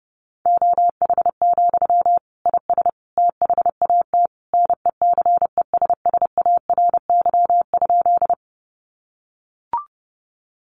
Morse code, heard as a keyed tone, about 30 wpm, 700 Hz. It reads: OH, IS THAT NECESSARY?
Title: Sentences from Top 800 Words 30wpm